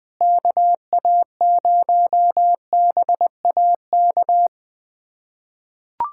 Morse code, heard as a keyed tone, about 19 wpm, 700 Hz. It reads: KA0BAK
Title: USA Callsign DE Callsign 20wpm